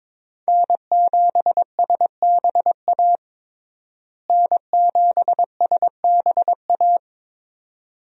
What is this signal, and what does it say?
Morse code, heard as a keyed tone, about 21 wpm, 700 Hz. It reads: N7SBA N7SBA
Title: USA Call Signs - 5 Characters - Repeated 3x - 22wpm